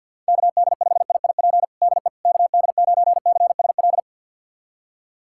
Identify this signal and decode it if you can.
Morse code, heard as a keyed tone, about 46 wpm, 700 Hz. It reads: KD5IIP DE KD9CSL
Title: USA Callsign DE Callsign - 6 Characters - 50wpm